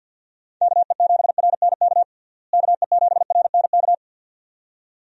Morse code, heard as a keyed tone, about 50 wpm, 700 Hz. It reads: KE7RNK KE7RNK
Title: USA Call Signs - 6 Characters - Repeated 2x - 50wpm